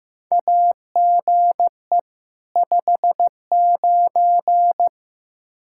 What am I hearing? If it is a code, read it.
Morse code, heard as a keyed tone, about 15 wpm, 700 Hz. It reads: AGE 59
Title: Practice QSO - Exchange 3 - More Info - 15wpm